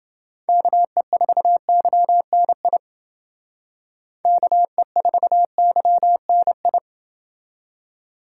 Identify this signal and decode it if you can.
Morse code, heard as a keyed tone, about 29 wpm, 700 Hz. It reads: KE4YNI KE4YNI